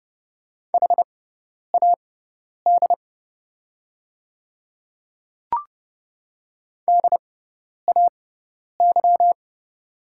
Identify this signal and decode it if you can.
Morse code, heard as a keyed tone, about 32 wpm, spaced as for 10 wpm, 690 Hz. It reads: HAD DAY
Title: 6x Character Spacing - Top 200 Words - 30wpm